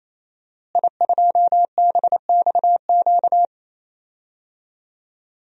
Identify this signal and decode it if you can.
Morse code, heard as a keyed tone, about 27 wpm, 700 Hz. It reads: I2BXQ
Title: DXCC-100 Call Signs - 5 Characters - Repeated 3x - 28wpm